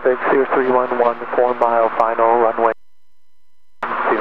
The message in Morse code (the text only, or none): none